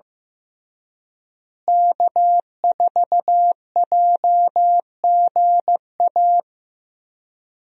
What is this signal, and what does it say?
Morse code, heard as a keyed tone, about 15 wpm, 700 Hz. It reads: K4JGA
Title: USA Call Signs - 5 Characters - Repeated 3x - 15wpm